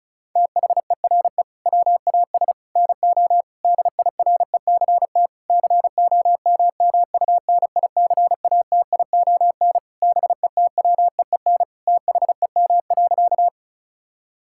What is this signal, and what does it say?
Morse code, heard as a keyed tone, about 36 wpm, 700 Hz. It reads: THERE WAS NO DIRECT COMMUNICATION BETWEEN THEM.
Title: Sentences from Top 1000 Words - Review Part 6 - 35wpm